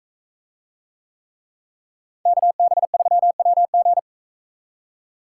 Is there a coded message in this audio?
Yes: Morse code from 2.3 to 4.0 s, heard as a keyed tone, about 43 wpm, 700 Hz. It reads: KB3WG